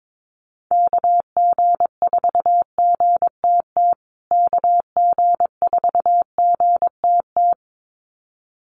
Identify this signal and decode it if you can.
Morse code, heard as a keyed tone, about 22 wpm, 700 Hz. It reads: KG4GTT KG4GTT
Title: USA Call Signs Repeated 2x 22wpm